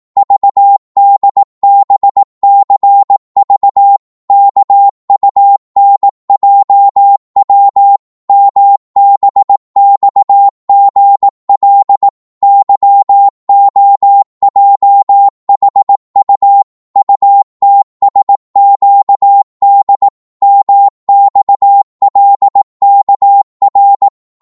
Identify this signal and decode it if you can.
Morse code, heard as a keyed tone, about 18 wpm, 800 Hz. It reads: VDBCV KUNJW MBXGL YOJHU UTSQD MXLKR
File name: Morse / ltr-18wpm